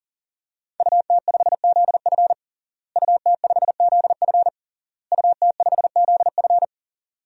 Morse code, heard as a keyed tone, about 40 wpm, 700 Hz. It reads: UT5ZF UT5ZF UT5ZF
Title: DXCC-100 Call Signs - 5 Characters - Repeated 3x - 40wpm